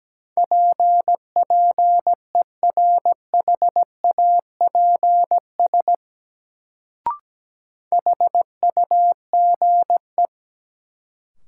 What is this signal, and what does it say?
Morse code, heard as a keyed tone, about 17 wpm, 700 Hz. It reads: PPERHAPS HUGE
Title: Sentences from Top 1100 Words - New Word Drill Alt - 17wpm